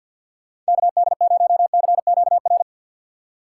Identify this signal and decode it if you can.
Morse code, heard as a keyed tone, about 53 wpm, 700 Hz. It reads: KD0CQR
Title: USA Call Signs - 6x Character Spacing - 50wpm